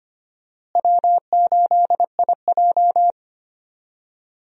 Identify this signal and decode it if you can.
Morse code, heard as a keyed tone, about 25 wpm, 700 Hz. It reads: W8IJ
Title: USA Call Signs - 4 Characters - Repeated 3x - 25wpm